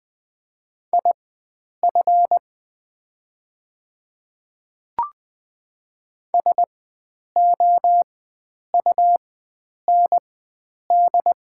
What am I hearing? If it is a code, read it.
Morse code, heard as a keyed tone, about 20 wpm, spaced as for 9 wpm, 700 Hz. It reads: IF SOUND
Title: 4x Character Spacing - Top 100 Words - 20wpm